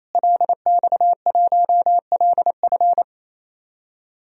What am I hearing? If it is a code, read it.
Morse code, heard as a keyed tone, about 28 wpm, 700 Hz. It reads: LX1LF
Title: DXCC-100 Call Signs - 5 Characters - Repeated 3x - 28wpm